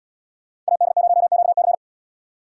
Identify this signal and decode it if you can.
Morse code, heard as a keyed tone, about 76 wpm, 700 Hz. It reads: AD0QC